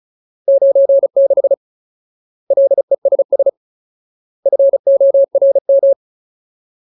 Morse code, heard as a keyed tone, about 34 wpm, 550 Hz. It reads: 96 LESS FORM
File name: Ultimate Word List / 35WPM Ultimate